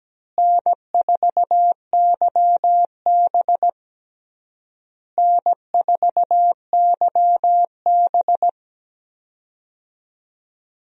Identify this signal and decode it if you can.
Morse code, heard as a keyed tone, about 17 wpm, 700 Hz. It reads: N4YB N4YB
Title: USA Call Signs - 4 Characters - Repeated 3x - 17wpm